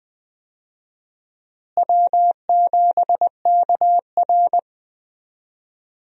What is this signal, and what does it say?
Morse code, heard as a keyed tone, about 20 wpm, 700 Hz. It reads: W7KR
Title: USA Call Signs - 4 Characters - Repeated 2x - 20wpm